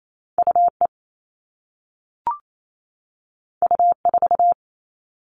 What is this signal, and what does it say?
Morse code, heard as a keyed tone, about 28 wpm, 700 Hz. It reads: UE U4
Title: All Cut Numbers - 2-Digit 28wpm